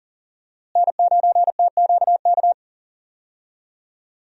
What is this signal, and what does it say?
Morse code, heard as a keyed tone, about 40 wpm, 700 Hz. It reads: N9TQK
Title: USA Call Signs - 5 Characters - Repeated 3x - 40wpm